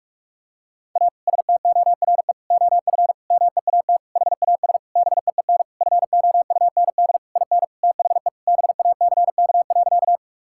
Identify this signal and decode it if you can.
Morse code, heard as a keyed tone, about 42 wpm, 700 Hz. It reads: A STORE OF MEAT HAS BEEN FOUND IN THE BACK.
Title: Sentences from Top 1000 Words - Review Part 5 - 45wpm